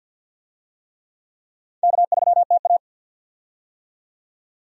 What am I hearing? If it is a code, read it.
Morse code, heard as a keyed tone, about 52 wpm, 700 Hz. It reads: K3TA
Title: USA Call Signs - 4 Characters - 50wpm